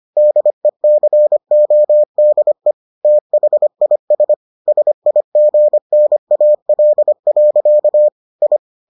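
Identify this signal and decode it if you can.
Morse code, heard as a keyed tone, about 25 wpm, 600 Hz. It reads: DECODE THIS SIGNAL. I
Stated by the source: Morse edition of the Foundations of Amateur Radio episodes.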